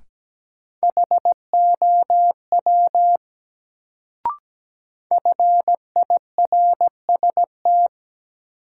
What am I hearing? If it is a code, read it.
Morse code, heard as a keyed tone, about 17 wpm, 700 Hz. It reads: HOW FIRST
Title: Sentences from Top 100 Words - New Word Drill - 17wpm